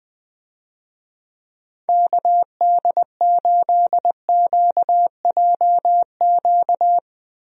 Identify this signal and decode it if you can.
Morse code, heard as a keyed tone, about 20 wpm, 700 Hz. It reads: KD8QJQ
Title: USA Call Signs - 6 Characters - Repeated 2x - 20wpm